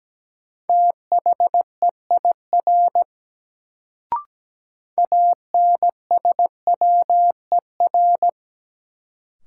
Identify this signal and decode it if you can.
Morse code, heard as a keyed tone, about 17 wpm, 700 Hz. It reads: THEIR ANSWER
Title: Sentences from Top 300 Words - New Word Drill - 17wpm